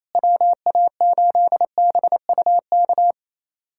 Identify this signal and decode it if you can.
Morse code, heard as a keyed tone, about 28 wpm, 700 Hz. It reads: WA8BUK
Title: USA Callsign DE Callsign - 6 Characters - 28wpm